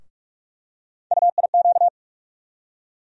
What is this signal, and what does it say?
Morse code, heard as a keyed tone, about 44 wpm, 700 Hz. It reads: UIQ
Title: USA Call Sign Postfixes - 3 Letters 45wpm